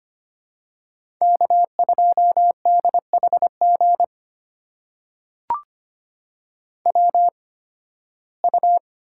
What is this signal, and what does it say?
Morse code, heard as a keyed tone, about 25 wpm, spaced as for 16 wpm, 700 Hz. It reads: K2DHG WU
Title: USA Call Signs - 8x Character Spacing - 25wpm